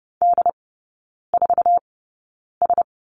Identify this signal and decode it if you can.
Morse code, heard as a keyed tone, about 30 wpm, spaced as for 9 wpm, 690 Hz. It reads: D4S
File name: Lesson 16e - Letter c - Three Characters 30wpm